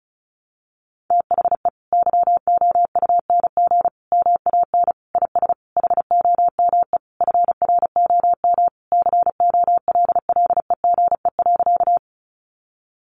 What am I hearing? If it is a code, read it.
Morse code, heard as a keyed tone, about 34 wpm, 700 Hz. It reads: THE YOUNG MAN IS HOME FROM COLLEGE.
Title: Sentences from Top 1300 Words 35wpm